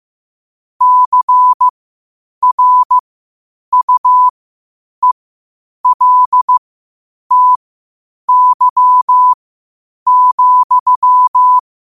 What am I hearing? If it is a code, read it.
Morse code, heard as a keyed tone, about 15 wpm, spaced as for 8 wpm, 1000 Hz. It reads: CRUELTY,
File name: qotdfiles / 8wpm